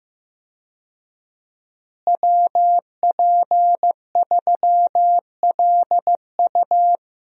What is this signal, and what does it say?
Morse code, heard as a keyed tone, about 16 wpm, 700 Hz. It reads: WP3LU
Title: USA Call Signs - 5 Characters - Repeated 2x - 15wpm